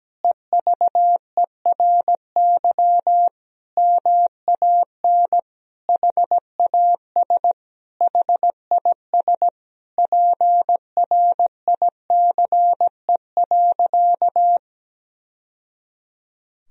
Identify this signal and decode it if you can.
Morse code, heard as a keyed tone, about 17 wpm, 700 Hz. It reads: EVERY MAN HAS HIS PRICE.
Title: Sentences from Top 1000 Words - Review Part 4 - 17wpm